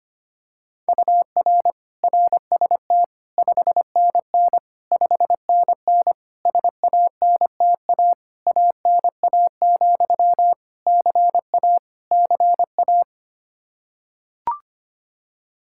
Morse code, heard as a keyed tone, about 25 wpm, 700 Hz. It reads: UR RST 5NN 5NN SANTA ANA, CA CA
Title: Practice QSO - Exchange 1 - Beginning - 25wpm